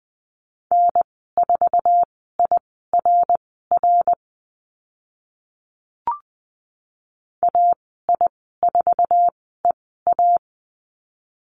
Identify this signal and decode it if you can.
Morse code, heard as a keyed tone, about 20 wpm, spaced as for 14 wpm, 700 Hz. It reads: N4IRR AI4EA
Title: 062 Lesson 11g - Letter h - Call Signs 20wpm